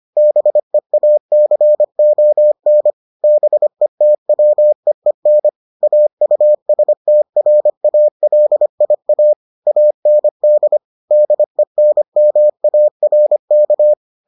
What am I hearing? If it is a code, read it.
Morse code, heard as a keyed tone, about 25 wpm, 600 Hz. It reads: BEACON BETWEEN AUSTRALIA AND DENMARK